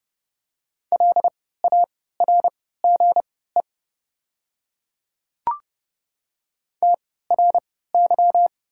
Morse code, heard as a keyed tone, about 30 wpm, spaced as for 17 wpm, 700 Hz. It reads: LARGE TRY
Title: 3x Character Spacing - Top 200 Words - 30wpm